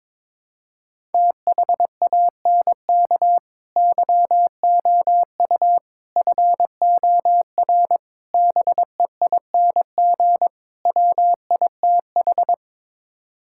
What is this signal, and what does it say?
Morse code, heard as a keyed tone, about 22 wpm, 700 Hz. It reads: THANK YOU FOR BEING WITH
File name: practice / 22wpm / Build A Phrase - 5-Word Phrases - 22wpm